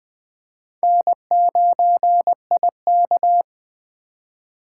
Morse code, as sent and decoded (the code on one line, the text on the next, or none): -. ----. .. -.-
N9IK